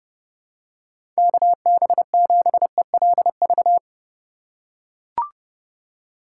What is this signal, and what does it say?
Morse code, heard as a keyed tone, about 32 wpm, 700 Hz. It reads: KB7ELV
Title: USA Call Signs - 6 Characters - 4x Character Spacing - 30wpm